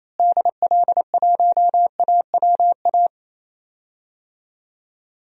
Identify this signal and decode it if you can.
Morse code, heard as a keyed tone, about 28 wpm, 700 Hz. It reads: DL1AWA
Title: DXCC-100 Call Signs - 6 Characters - Repeated 3x - 28wpm